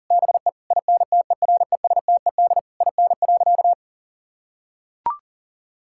Morse code, heard as a keyed tone, about 40 wpm, 700 Hz. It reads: BE INTERESTED IN.
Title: Sentences from Top 1000 Words - Review Part 6 - 40wpm